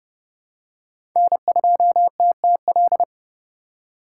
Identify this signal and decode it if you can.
Morse code, heard as a keyed tone, about 32 wpm, 700 Hz. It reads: N2TTL
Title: USA Call Signs - 5 Characters - 3x Character Spacing - 30wpm